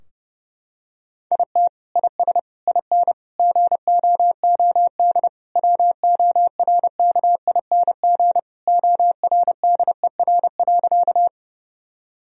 Morse code, heard as a keyed tone, about 30 wpm, 700 Hz. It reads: IT IS IN GOOD WORKING ORDER.
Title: Sentences from Top 1000 Words - Review Part 4 - 30wpm